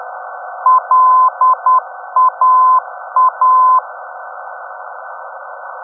Morse code, heard as a keyed tone, about 10 wpm, 1020 Hz. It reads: LAA
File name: europe / d-5078- dec 03 - ndb laa-ti